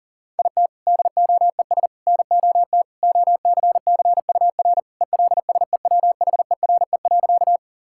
Morse code, heard as a keyed tone, about 40 wpm, 700 Hz. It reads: IT DOES NOT OCCUR ELSEWHERE.
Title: Sentences from Top 1600 Words 40wpm